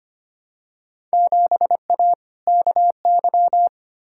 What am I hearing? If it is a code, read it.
Morse code, heard as a keyed tone, about 25 wpm, 700 Hz. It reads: 7A KY
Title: Field Day Exchanges Encore - 25wpm